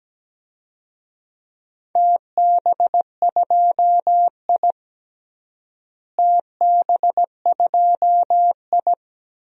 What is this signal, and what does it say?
Morse code, heard as a keyed tone, about 17 wpm, 700 Hz. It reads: TB2I TB2I
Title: DXCC-100 Call Signs - 4 Characters - Repeated 3x - 17wpm